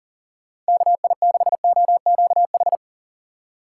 Morse code, heard as a keyed tone, about 40 wpm, 700 Hz. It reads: KI6OQH
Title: USA Call Signs - 6 Characters - 4x Character Spacing - 40wpm